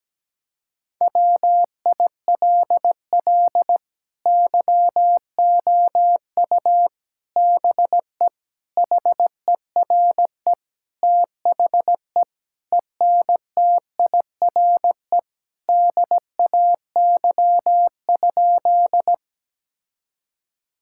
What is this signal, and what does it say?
Morse code, heard as a keyed tone, about 17 wpm, 700 Hz. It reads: WILL YOU BE HERE THE ENTIRE DAY?
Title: Sentences from Top 1000 Words - Review Part 5 - 17wpm